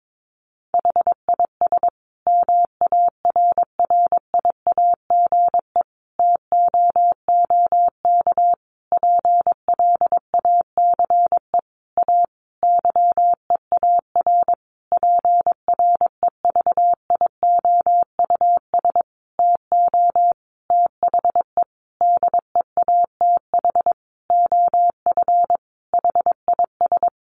Morse code, heard as a keyed tone, about 22 wpm, 700 Hz. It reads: HIS MARRIAGE TOOK PLACE A YEAR PREVIOUS TO THE DEATH OF HIS
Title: Sentences from Top 1500 Words 22wpm